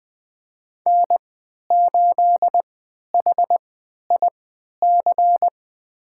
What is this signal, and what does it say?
Morse code, heard as a keyed tone, about 21 wpm, spaced as for 11 wpm, 700 Hz. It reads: N8HIC
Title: USA Call Signs - 5 Characters - 3x Character Spacing - 20wpm